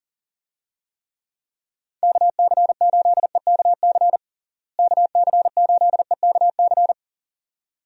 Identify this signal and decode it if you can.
Morse code, heard as a keyed tone, about 40 wpm, 700 Hz. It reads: KC8EKC KC8EKC
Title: USA Call Signs - 6 Characters - Repeated 2x - 40wpm